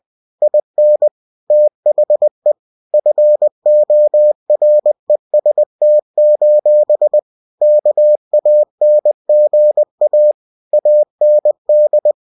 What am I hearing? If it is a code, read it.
Morse code, heard as a keyed tone, about 20 wpm, 600 Hz. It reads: IN THE FOREST: KANGA AND